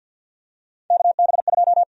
Morse code, heard as a keyed tone, about 53 wpm, 700 Hz. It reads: KB2
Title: USA Call Sign Prefixes - 50wpm